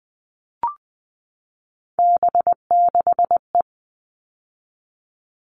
Morse code, heard as a keyed tone, about 20 wpm, 700 Hz. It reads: B6E
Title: All Cut Numbers - 3-Digit 20wpm